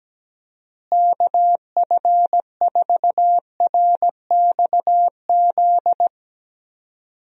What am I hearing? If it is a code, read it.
Morse code, heard as a keyed tone, about 17 wpm, 700 Hz. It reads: KF4RXZ
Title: USA Call Signs - 6 Characters - Repeated 3x - 17wpm